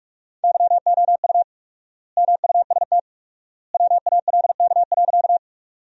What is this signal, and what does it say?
Morse code, heard as a keyed tone, about 46 wpm, 700 Hz. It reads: YOU MUST WALK.
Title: 4x Word-Spacing - Sentences from Top 300 Words - 45wpm